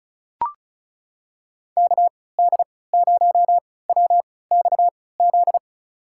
Morse code, heard as a keyed tone, about 36 wpm, spaced as for 20 wpm, 700 Hz. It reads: KD0WXZ
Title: USA Call Signs - 6 Characters - 3x Character Spacing - 35wpm